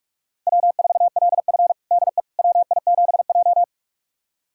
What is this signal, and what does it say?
Morse code, heard as a keyed tone, about 44 wpm, 690 Hz. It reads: W4LF DE WI7J
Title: USA Callsign DE Callsign - 4 Characters - 45wpm